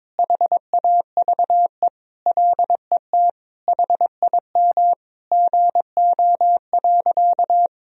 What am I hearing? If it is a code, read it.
Morse code, heard as a keyed tone, about 22 wpm, 700 Hz. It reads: HAVE LET HIM GO.
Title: Sentences from Top 1000 Words - Review Part 1 - 22wpm